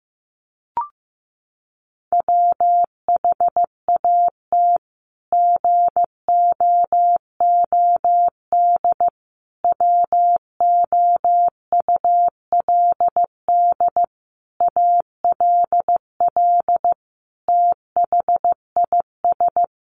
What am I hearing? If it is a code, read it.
Morse code, heard as a keyed tone, about 15 wpm, 700 Hz. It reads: WHAT GOOD WOULD ALL THIS
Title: Sentences from Top 1200 Words 15wpm